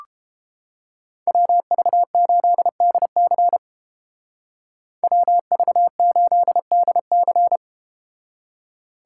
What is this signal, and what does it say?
Morse code, heard as a keyed tone, about 32 wpm, 700 Hz. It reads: WV8DC WV8DC